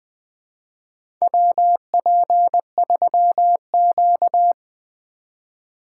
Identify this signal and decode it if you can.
Morse code, heard as a keyed tone, about 20 wpm, 700 Hz. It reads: WP3Q
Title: USA Call Signs - 4 Characters - 2x Character Spacing - 20wpm